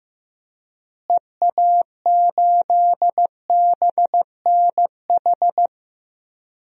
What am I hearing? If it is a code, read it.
Morse code, heard as a keyed tone, about 16 wpm, 700 Hz. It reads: EA8BNH
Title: DXCC-100 Call Signs - 6 Characters - Repeated 3x - 15wpm